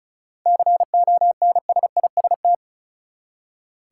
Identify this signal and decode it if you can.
Morse code, heard as a keyed tone, about 36 wpm, 700 Hz. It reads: CONSIST
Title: Seven Letter Words - 35wpm